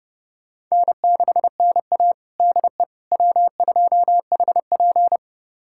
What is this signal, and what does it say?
Morse code, heard as a keyed tone, about 30 wpm, 700 Hz. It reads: N6NA DE W2HP
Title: USA Callsign DE Callsign - 4 Characters - 30wpm